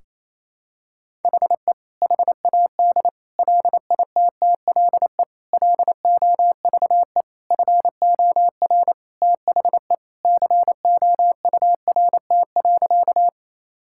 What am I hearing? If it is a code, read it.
Morse code, heard as a keyed tone, about 28 wpm, 700 Hz. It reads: HE HAD LITTLE LOVE FOR THE COURT.
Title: Sentences from Top 1600 Words 28wpm